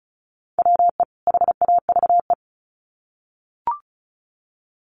Morse code, heard as a keyed tone, about 34 wpm, 700 Hz. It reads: WE HAVE
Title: Sets of 2 Words 35wpm